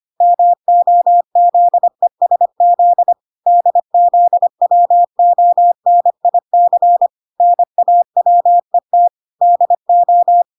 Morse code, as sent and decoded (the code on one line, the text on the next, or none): -- --- --.. ... --.. / -.. --.. .-- --- -. .. -.-. / -. .- .-- . - / -.. ---
MOZESZ DZWONIC NAWET DO